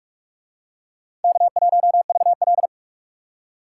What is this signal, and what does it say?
Morse code, heard as a keyed tone, about 44 wpm, 700 Hz. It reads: K1VL